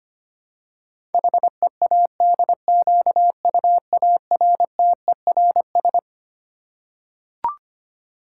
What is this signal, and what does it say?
Morse code, heard as a keyed tone, about 25 wpm, 700 Hz. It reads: HEADQUARTERS
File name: Twelve Letter Words - 25wpm